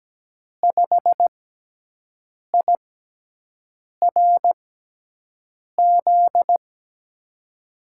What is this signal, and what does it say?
Morse code, heard as a keyed tone, about 17 wpm, spaced as for 6 wpm, 700 Hz. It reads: 5IRZ
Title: USA Call Signs - 6x Character Spacing - 17wpm